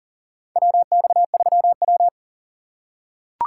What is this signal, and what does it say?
Morse code, heard as a keyed tone, about 38 wpm, 700 Hz. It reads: WX3W
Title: USA Call Signs - 4 Characters - 4x Character Spacing - 40wpm